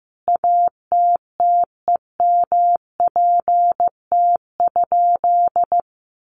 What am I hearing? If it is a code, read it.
Morse code, heard as a keyed tone, about 15 wpm, 700 Hz. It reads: ATTEMPT?
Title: Sentences from Top 1000 Words 15wpm